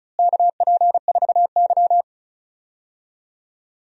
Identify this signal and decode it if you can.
Morse code, heard as a keyed tone, about 34 wpm, 700 Hz. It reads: KP4Y
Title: DXCC-100 Call Signs - 4 Characters - Repeated 3x - 35wpm